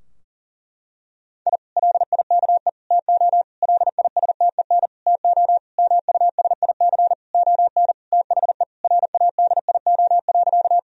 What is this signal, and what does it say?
Morse code, heard as a keyed tone, about 40 wpm, 700 Hz. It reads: I LIKE TO LISTEN TO MUSIC ON THE RADIO.
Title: Sentences from Top 1000 Words - Review Part 3 - 40wpm